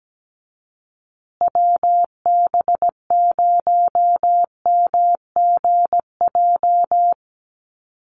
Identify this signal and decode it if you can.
Morse code, heard as a keyed tone, about 17 wpm, 700 Hz. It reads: WB0MGJ
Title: USA Call Signs Repeated 2x 17wpm